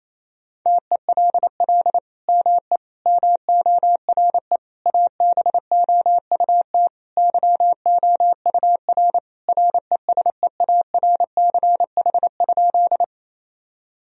Morse code, heard as a keyed tone, about 28 wpm, 700 Hz. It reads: TELL ME MORE ABOUT YOUR RESEARCH?
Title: Sentences from Top 1000 Words - Review Part 4 - 28wpm